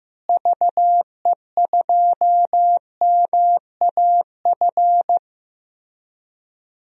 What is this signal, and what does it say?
Morse code, heard as a keyed tone, about 15 wpm, 700 Hz. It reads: VE2MAF
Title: DXCC-100 Call Signs - 6 Characters - Repeated 3x - 15wpm